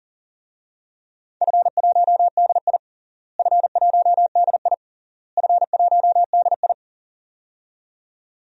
Morse code, heard as a keyed tone, about 40 wpm, 700 Hz. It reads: F1DI F1DI F1DI
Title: DXCC-100 Call Signs - 4 Characters - Repeated 3x - 40wpm